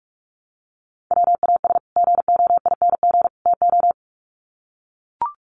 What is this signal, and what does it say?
Morse code, heard as a keyed tone, about 44 wpm, 700 Hz. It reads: WAS GOING TO
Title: Sets of 3 Words 45wpm